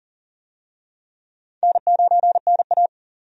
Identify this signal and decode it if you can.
Morse code, heard as a keyed tone, about 40 wpm, 700 Hz. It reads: N9NA
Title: USA Call Signs - 4 Characters - 40wpm